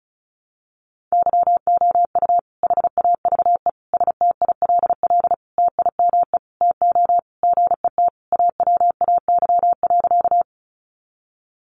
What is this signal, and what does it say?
Morse code, heard as a keyed tone, about 34 wpm, 700 Hz. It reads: YOU HAVE STILL TIME TO GET AWAY.
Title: Sentences from Top 600 Words 35wpm